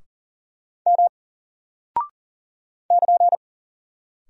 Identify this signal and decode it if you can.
Morse code, heard as a keyed tone, about 39 wpm, 700 Hz. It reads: M <KN>
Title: Single Character (Letter:Number:Punctuation:Prosign) 40wpm